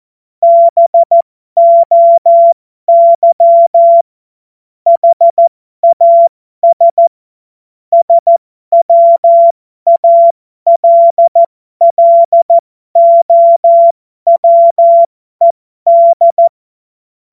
Morse code, heard as a keyed tone, about 14 wpm, spaced as for 12 wpm, 680 Hz. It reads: BOY HAS SWALLOWED
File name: Morsekurs / Witze